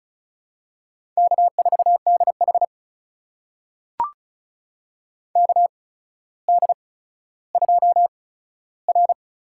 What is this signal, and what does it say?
Morse code, heard as a keyed tone, about 34 wpm, spaced as for 15 wpm, 700 Hz. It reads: K4DH KD2R
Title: USA Call Signs - 8x Character Spacing - 35wpm